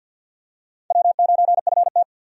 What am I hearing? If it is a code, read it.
Morse code, heard as a keyed tone, about 49 wpm, 700 Hz. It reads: W9FT
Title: USA Call Signs - 4 Characters - 50wpm